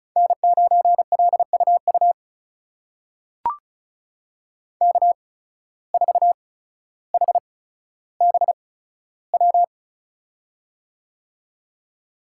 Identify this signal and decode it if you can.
Morse code, heard as a keyed tone, about 34 wpm, spaced as for 15 wpm, 700 Hz. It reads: N9LUU K4HBW
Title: USA Call Signs - 5 Characters - 8x Character Spacing - 35wpm